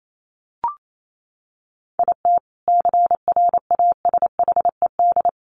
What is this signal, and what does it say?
Morse code, heard as a keyed tone, about 28 wpm, 700 Hz. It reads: IT CRASHED
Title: A Sound Of Thunder 28wpm